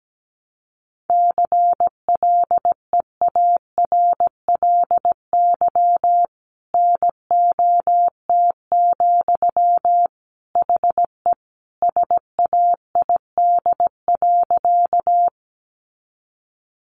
Morse code, heard as a keyed tone, about 17 wpm, 700 Hz. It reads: CLEARLY NOT, HE SAID.
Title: Sentences from Top 1300 Words 17wpm